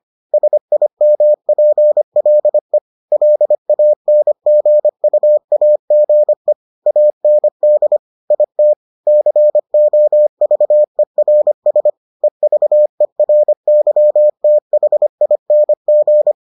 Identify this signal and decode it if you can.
Morse code, heard as a keyed tone, about 25 wpm, 600 Hz. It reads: SIMPLE LANGUAGE AND IT COVERS EVERYTHING